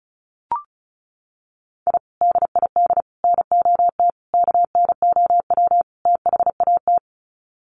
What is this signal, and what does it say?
Morse code, heard as a keyed tone, about 34 wpm, 700 Hz. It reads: I DID NOT KNOW THAT
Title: Sets of 5 Words 35wpm